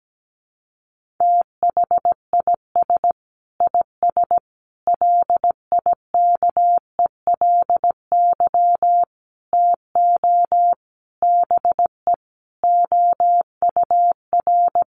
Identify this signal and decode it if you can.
Morse code, heard as a keyed tone, about 17 wpm, 700 Hz. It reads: THIS IS LIKELY TO BE OUR
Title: Sentences from Top 1400 Words 17wpm